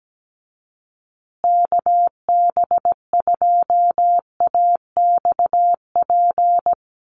Morse code, heard as a keyed tone, about 17 wpm, 700 Hz. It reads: KB2AXP
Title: USA Call Signs Repeated 2x 17wpm